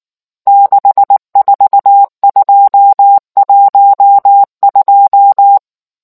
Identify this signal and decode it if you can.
Morse code, heard as a keyed tone, about 19 wpm, 800 Hz. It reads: 64212